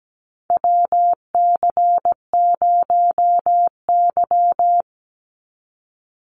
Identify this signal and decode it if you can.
Morse code, heard as a keyed tone, about 17 wpm, 700 Hz. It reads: WC0Y
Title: NA SOTA Chaser Callsigns 17wpm